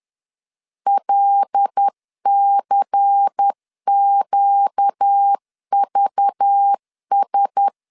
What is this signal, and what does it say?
Morse code, heard as a keyed tone, about 10 wpm, 800 Hz. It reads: LCQVS
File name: LEZIONI / 7 GF9Z / 7 GF 10wpm 2514